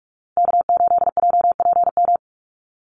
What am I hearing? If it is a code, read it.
Morse code, heard as a keyed tone, about 44 wpm, 700 Hz. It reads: K8JPM
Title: Call Signs Repeated 2x - 45wpm